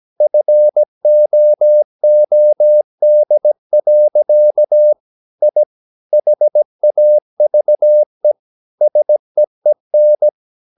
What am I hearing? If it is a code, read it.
Morse code, heard as a keyed tone, about 17 wpm, 600 Hz. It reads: FOOD. I HAVE SEEN
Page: Romanen Never af Ken Follet på engelsk.